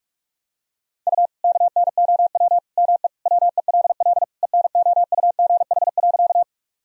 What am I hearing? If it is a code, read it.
Morse code, heard as a keyed tone, about 41 wpm, 700 Hz. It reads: U KNOW ME WELL ENOUGH.
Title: Sentences from Top 400 Words and Abbreviations 45wpm